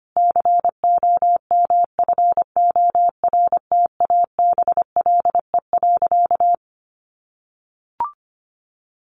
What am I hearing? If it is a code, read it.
Morse code, heard as a keyed tone, about 25 wpm, 700 Hz. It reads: COMFORTABLE.
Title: Sentences from Top 1500 Words 25wpm